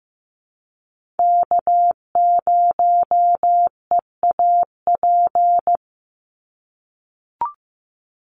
Morse code, heard as a keyed tone, about 15 wpm, 700 Hz. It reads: K0EAP
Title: NA SOTA Chaser Callsigns 15wpm